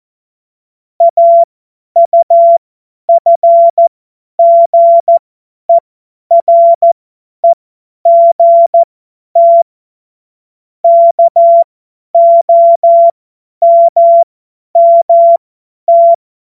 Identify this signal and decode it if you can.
Morse code, heard as a keyed tone, about 14 wpm, spaced as for 10 wpm, 680 Hz. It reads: AUFGEREGT KOMMT
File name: Morsekurs / Witze / Witz 23. Mai 2023 50 F70